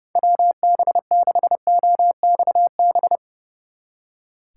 Morse code, heard as a keyed tone, about 36 wpm, 700 Hz. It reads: WB6OXB
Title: Speed Racing - USA 2x3 Call Signs 30wpm